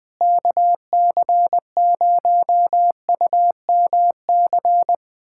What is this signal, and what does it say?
Morse code, heard as a keyed tone, about 21 wpm, 700 Hz. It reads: KC0UMC
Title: USA Callsign DE Callsign - 6 Characters - 20wpm